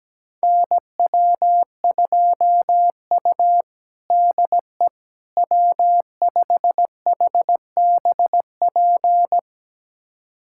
Morse code, heard as a keyed tone, about 17 wpm, 700 Hz. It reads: NW2U DE W5HBP
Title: USA Callsign DE Callsign 17wpm